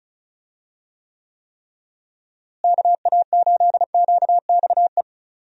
Morse code, heard as a keyed tone, about 36 wpm, 700 Hz. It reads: KA8QXE
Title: USA Call Signs - 6 Characters - Repeated 3x - 35wpm